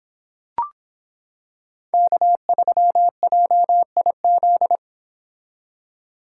Morse code, heard as a keyed tone, about 26 wpm, 700 Hz. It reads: K3JIZ